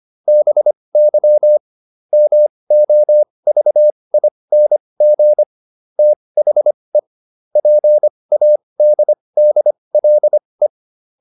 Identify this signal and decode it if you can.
Morse code, heard as a keyed tone, about 25 wpm, spaced as for 20 wpm, 600 Hz. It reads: BY MOVING THE PADDLE
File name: SS215 20